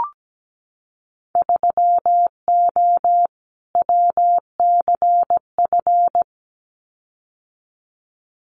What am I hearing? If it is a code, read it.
Morse code, heard as a keyed tone, about 17 wpm, 700 Hz. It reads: 3O WCF
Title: Winter Field Day Exchanges 17wpm